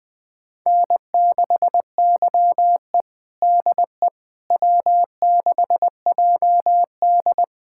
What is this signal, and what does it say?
Morse code, heard as a keyed tone, about 21 wpm, 700 Hz. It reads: N6YE DE W6JD
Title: USA Callsign DE Callsign - 4 Characters - 20wpm